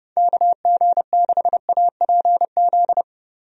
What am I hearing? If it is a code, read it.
Morse code, heard as a keyed tone, about 30 wpm, 700 Hz. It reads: KG6APZ
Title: USA Call Signs - 6 Characters - Repeated 3x - 30wpm